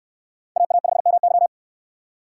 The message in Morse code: AA5MY

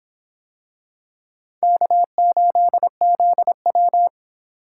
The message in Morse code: K8ZW

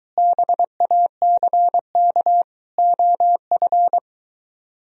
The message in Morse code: BACK OF

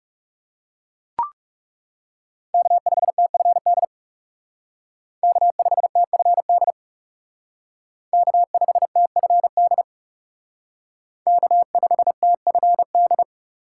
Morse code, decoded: K5TFD K5TFD K5TFD K5TFD

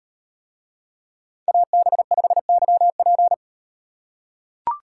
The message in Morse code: AB5YP

